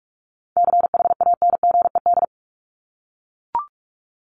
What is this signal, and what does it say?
Morse code, heard as a keyed tone, about 44 wpm, 700 Hz. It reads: CHANGED
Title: ED Suffix - Top 100 Words with Base Words 45wpm